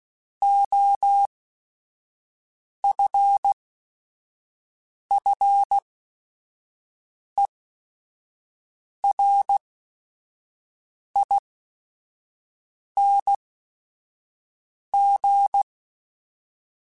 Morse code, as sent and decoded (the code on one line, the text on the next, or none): --- ..-. ..-. .-. .. -. --.
OFFERING